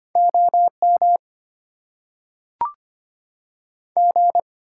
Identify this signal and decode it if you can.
Morse code, heard as a keyed tone, about 25 wpm, 700 Hz. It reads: OM G
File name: Top 90 QSO Elements - 8x Character Spacing - 25wpm